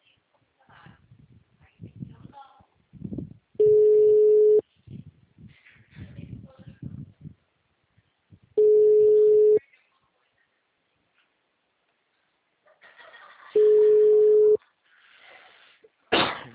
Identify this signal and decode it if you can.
Ringback tone (Europe, 425 Hz), with speech.